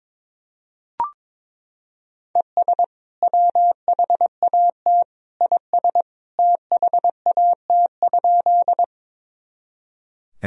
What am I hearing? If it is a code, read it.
Morse code, heard as a keyed tone, about 22 wpm, 700 Hz. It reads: ES WHAT IS THAT?
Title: Sentences from Top 100 Words and Abbreviations 22wpm